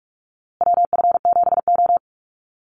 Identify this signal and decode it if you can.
Morse code, heard as a keyed tone, about 44 wpm, 690 Hz. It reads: WF7O
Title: USA Call Signs Repeated 2x 45wpm